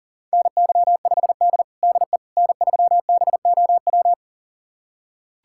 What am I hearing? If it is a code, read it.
Morse code, heard as a keyed tone, about 40 wpm, 700 Hz. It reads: NY5D DE N3BOW
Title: USA Callsign DE Callsign 40wpm